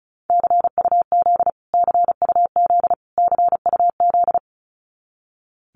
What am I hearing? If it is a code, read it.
Morse code, heard as a keyed tone, about 34 wpm, 700 Hz. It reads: CUZ CUZ CUZ
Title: Abbreviations Repeated 3x - 35wpm